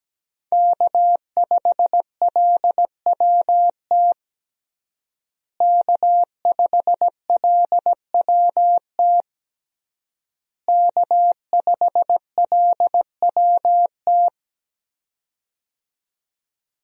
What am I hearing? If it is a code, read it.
Morse code, heard as a keyed tone, about 17 wpm, 700 Hz. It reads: K5LWT K5LWT K5LWT
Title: USA Call Signs - 5 Characters - Repeated 3x - 17wpm